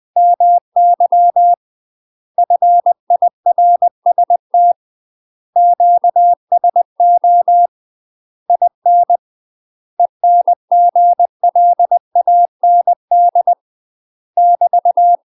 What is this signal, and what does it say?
Morse code, heard as a keyed tone, about 20 wpm, 700 Hz. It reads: MY FIRST QSO IN ENGLAND <BT>
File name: UK MASTER QSOs TRAINING / 20WPM / UK MASTER QSOs 0066